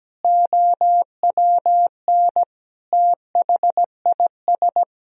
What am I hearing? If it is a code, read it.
Morse code, heard as a keyed tone, about 17 wpm, 700 Hz. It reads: OWN THIS
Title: Sentences from Top 1000 Words - Review Part 4 - 17wpm